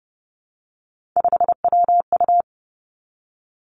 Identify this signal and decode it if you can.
Morse code, heard as a keyed tone, about 30 wpm, 700 Hz. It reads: 5WU
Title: Three Challenging Letter-Number Combinations 30wpm